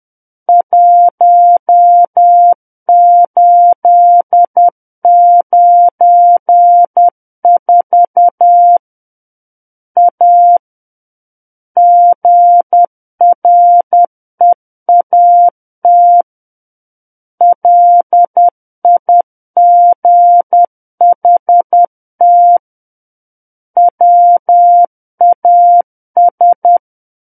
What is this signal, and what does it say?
Morse code, heard as a keyed tone, about 10 wpm, 700 Hz. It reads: 1894 A GREAT LIGHT WAS
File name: Book One / 01-Chapter 1 - 10 WPM